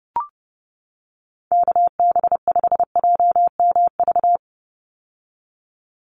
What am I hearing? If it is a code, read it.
Morse code, heard as a keyed tone, about 29 wpm, 700 Hz. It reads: KB5JMV
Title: USA Call Signs 30wpm